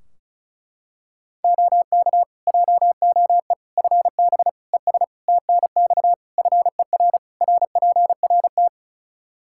Morse code, heard as a keyed tone, about 36 wpm, 700 Hz. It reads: OK JOE FB ES TNX FER RPRT
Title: K1USN SST QSO Practice 35wpm